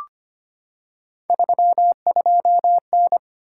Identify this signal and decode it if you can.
Morse code, heard as a keyed tone, about 25 wpm, 700 Hz. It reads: 32N
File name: Three Letter-Number Combinations - 25wpm